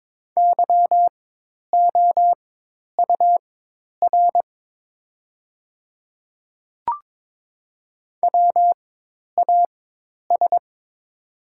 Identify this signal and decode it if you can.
Morse code, heard as a keyed tone, about 23 wpm, spaced as for 10 wpm, 700 Hz. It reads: YOUR WAS
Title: 4x Character Spacing - Top 100 Words - 22wpm